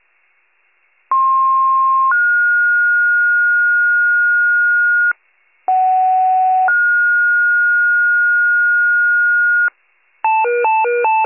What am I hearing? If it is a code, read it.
Two-tone page, heard as a one-second tone, then a three-second tone, about 1060 then 1430 Hz, about 750 then 1430 Hz. It is heard from 1.1 to 9.7 s.